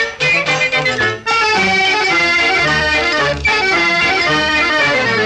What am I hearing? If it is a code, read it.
Emergency Alert System (SAME) tones, heard as harsh buzzing bursts. It is heard from 0.5 to 4.2 s.